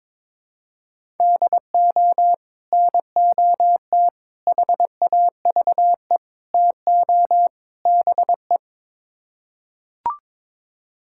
Morse code, heard as a keyed tone, about 21 wpm, 700 Hz. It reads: DO NOT HAVE TO BE
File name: practice / 22wpm / Build A Phrase - 5-Word Phrases - 22wpm